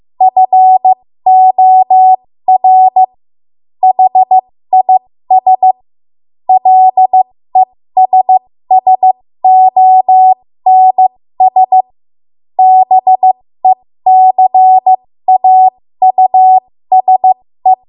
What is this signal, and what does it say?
Morse code, heard as a keyed tone, about 15 wpm, spaced as for 13 wpm, 750 Hz. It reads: FOR HIS LESSONS BECAUSE
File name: Morse / 13WPM